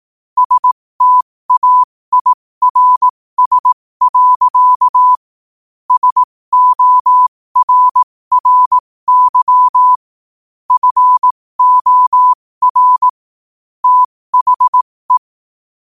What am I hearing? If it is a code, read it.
Morse code, heard as a keyed tone, about 18 wpm, spaced as for 15 wpm, 1000 Hz. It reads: STAIRS. SORRY FOR THE
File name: qotdfiles / 15wpm